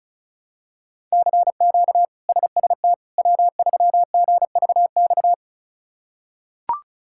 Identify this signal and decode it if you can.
Morse code, heard as a keyed tone, about 36 wpm, 700 Hz. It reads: CQ SST W3GVX
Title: K1USN SST QSO Practice - 35wpm